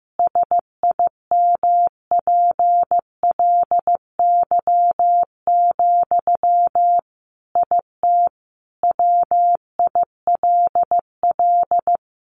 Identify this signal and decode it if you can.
Morse code, heard as a keyed tone, about 15 wpm, 700 Hz. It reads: SIMPLY, IT WILL
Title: Sentences from Top 800 Words 15wpm